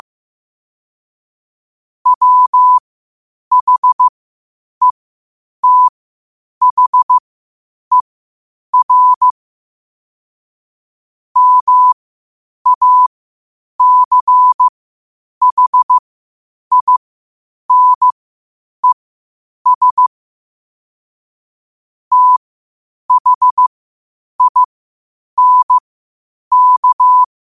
Morse code, heard as a keyed tone, about 15 wpm, spaced as for 8 wpm, 1000 Hz. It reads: WHETHER MACHINES THINK
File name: qotdfiles / 8wpm